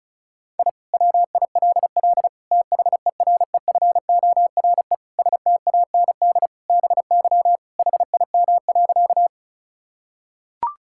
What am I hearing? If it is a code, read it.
Morse code, heard as a keyed tone, about 36 wpm, 700 Hz. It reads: I WILL THEREFORE STAND BY HIM.
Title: Sentences from Top 1000 Words - Review Part 5 - 35wpm